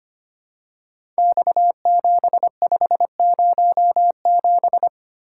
Morse code, heard as a keyed tone, about 25 wpm, 700 Hz. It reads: X7507
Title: Code Groups - 5 Letter:Numbers - 25wpm